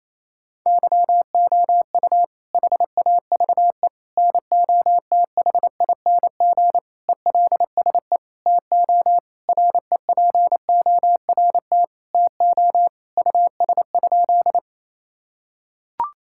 Morse code, heard as a keyed tone, about 28 wpm, 700 Hz. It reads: YOU HAVE NOTHING ELSE TO REPORT TO US?
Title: Sentences from Top 1000 Words - Review Part 6 - 28wpm